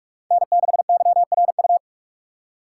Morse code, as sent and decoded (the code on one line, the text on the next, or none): -. -.... -.-- .-. ..-
N6YRU